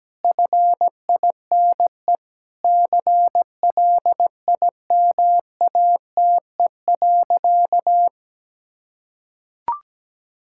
Morse code, heard as a keyed tone, about 17 wpm, 700 Hz. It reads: FINE CLIMATE.
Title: Sentences from Top 1600 Words 17wpm